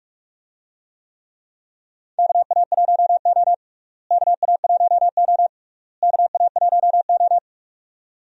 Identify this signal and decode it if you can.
Morse code, heard as a keyed tone, about 46 wpm, 700 Hz. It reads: KA1O KA1O KA1O
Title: USA Call Signs - 4 Characters - Repeated 3x - 45wpm